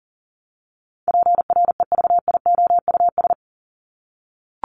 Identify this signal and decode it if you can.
Morse code, heard as a keyed tone, about 40 wpm, 700 Hz. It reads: PREVIOUS
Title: LY Suffix - Top 100 Words with Base Words 40wpm